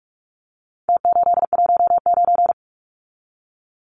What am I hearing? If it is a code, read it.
Morse code, heard as a keyed tone, about 44 wpm, 700 Hz. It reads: T819
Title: Numbers - Contest Style - 4-Digit 45wpm